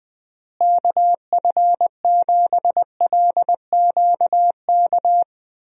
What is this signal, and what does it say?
Morse code, heard as a keyed tone, about 20 wpm, 700 Hz. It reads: KF7LQK
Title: USA Call Signs - 6 Characters - Repeated 3x - 20wpm